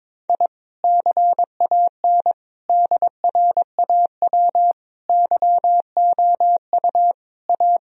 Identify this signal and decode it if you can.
Morse code, heard as a keyed tone, about 22 wpm, 700 Hz. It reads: I CAN DRAW YOU A
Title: Sentences from Top 1000 Words - Review Part 1 - 22wpm